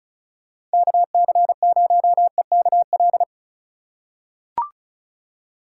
Morse code, heard as a keyed tone, about 36 wpm, 700 Hz. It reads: KC0EKL
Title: USA Call Signs - 6 Characters - 4x Character Spacing - 35wpm